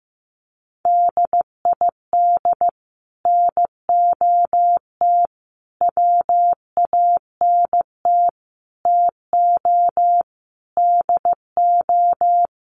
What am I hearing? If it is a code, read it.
Morse code, heard as a keyed tone, about 15 wpm, 700 Hz. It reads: DID NOT WANT TO DO
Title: Sets of 5 Words Encore 15wpm